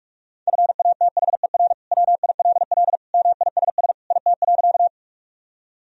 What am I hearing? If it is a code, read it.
Morse code, heard as a keyed tone, about 46 wpm, 700 Hz. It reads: FATHER WILL MISS IT.
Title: Sentences from Top 1000 Words - Review Part 3 - 45wpm